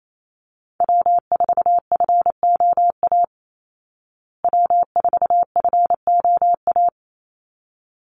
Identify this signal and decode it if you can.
Morse code, heard as a keyed tone, about 28 wpm, 700 Hz. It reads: W4FOA W4FOA
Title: Call Signs Repeated 3x - 28wpm